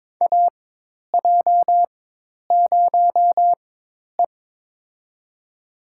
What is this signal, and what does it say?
Morse code, heard as a keyed tone, about 22 wpm, spaced as for 10 wpm, 700 Hz. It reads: AJ0E
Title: USA Call Signs - 4 Characters - 4x Character Spacing - 22wpm